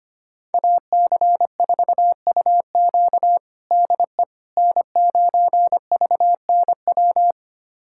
Morse code, heard as a keyed tone, about 25 wpm, 700 Hz. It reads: AC4UQ DE N9VNW
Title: USA Callsign DE Callsign - 5 Characters - 25wpm